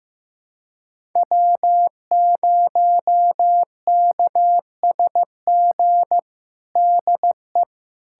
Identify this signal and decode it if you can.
Morse code, heard as a keyed tone, about 16 wpm, 700 Hz. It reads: W0KSG DE
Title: USA Callsign DE Callsign - 5 Characters - 15wpm